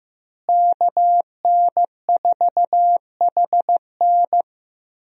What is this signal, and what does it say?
Morse code, heard as a keyed tone, about 15 wpm, 700 Hz. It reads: KN4HN
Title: USA Call Signs - 5 Characters - Repeated 3x - 15wpm